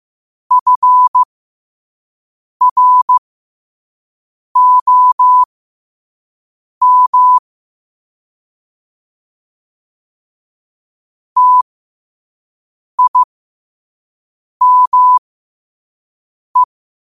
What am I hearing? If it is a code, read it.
Morse code, heard as a keyed tone, about 15 wpm, spaced as for 5 wpm, 1000 Hz. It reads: FROM TIME